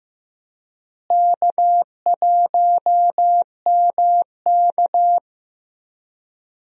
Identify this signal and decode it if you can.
Morse code, heard as a keyed tone, about 15 wpm, 700 Hz. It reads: K1MK
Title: USA Call Signs - 4 Characters - Repeated 3x - 15wpm